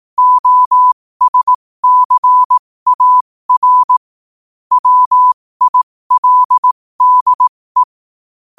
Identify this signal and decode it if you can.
Morse code, heard as a keyed tone, about 18 wpm, spaced as for 15 wpm, 1000 Hz. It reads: OSCAR WILDE
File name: qotdfiles / 15wpm